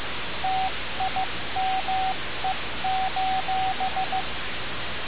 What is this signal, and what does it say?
Morse code, heard as a keyed tone, about 15 wpm, 760 Hz, with radio noise. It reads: TIME:
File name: radiocap / REC z